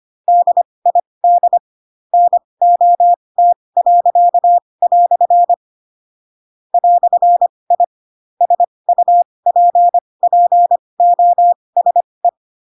Morse code, heard as a keyed tone, about 25 wpm, spaced as for 20 wpm, 700 Hz. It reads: DID NOT." "I SUPPOSE
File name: Pinocchio / Pinocchio ch2 15 25 ik1hge